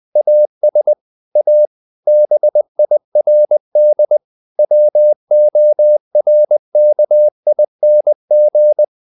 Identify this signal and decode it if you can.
Morse code, heard as a keyed tone, about 20 wpm, 600 Hz. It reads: AS A BIRD WORKING
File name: the old man and the sea / the old man and the sea part 1 20wpm